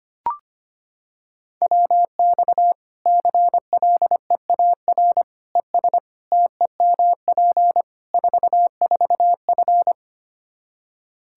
Morse code, heard as a keyed tone, about 25 wpm, 700 Hz. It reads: WX CLEAR ES TEMP 44F
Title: Practice QSO - Exchange 2 - Info - 25wpm